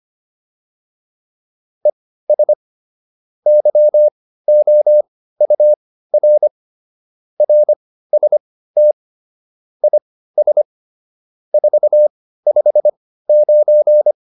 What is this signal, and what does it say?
Morse code, heard as a keyed tone, about 25 wpm, spaced as for 15 wpm, 600 Hz. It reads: ES YOUR RST IS 459